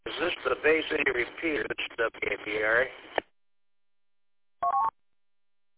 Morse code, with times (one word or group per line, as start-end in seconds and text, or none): none